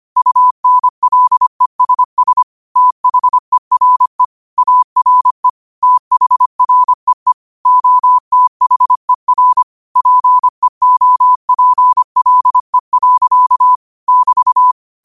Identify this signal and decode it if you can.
Morse code, heard as a keyed tone, about 25 wpm, 1000 Hz. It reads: UNLESS THERE ARE THREE OTHER PEOPLE. <BT>
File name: qotdfiles / 25wpm